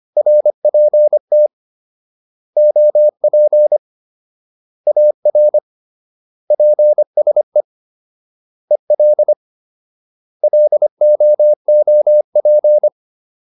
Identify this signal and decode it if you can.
Morse code, heard as a keyed tone, about 25 wpm, 600 Hz. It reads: RPT OP AR PSE EL LOOP